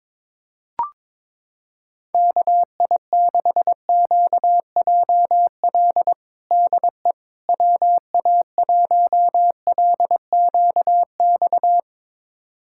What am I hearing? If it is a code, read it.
Morse code, heard as a keyed tone, about 22 wpm, 700 Hz. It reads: KI6QJL DE WA1LQX
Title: USA Callsign DE Callsign - 6 Characters - 22wpm